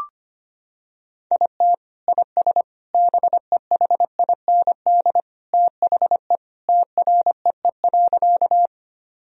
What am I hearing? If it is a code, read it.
Morse code, heard as a keyed tone, about 25 wpm, 700 Hz. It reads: IT IS BEHIND THE TREE.
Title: Sentences from Top 1000 Words - Review Part 1 - 25wpm